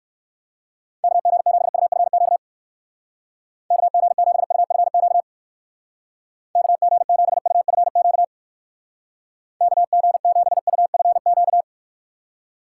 Morse code, heard as a keyed tone, about 57 wpm, 700 Hz. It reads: KG7UFQ KG7UFQ KG7UFQ KG7UFQ